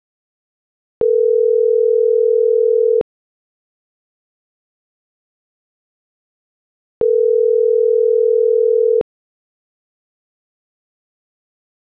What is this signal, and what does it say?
Ringback tone (North America), 440 + 480 Hz.